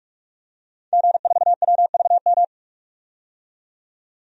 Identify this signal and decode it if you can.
Morse code, heard as a keyed tone, about 41 wpm, 700 Hz. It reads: G4WVM
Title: DXCC-100 Call Signs - 5 Characters - 45wpm